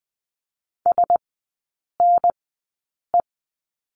Morse code, heard as a keyed tone, about 20 wpm, spaced as for 8 wpm, 700 Hz. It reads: SNE
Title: 035 Lesson 7e - Letter s - Three Characters 20wpm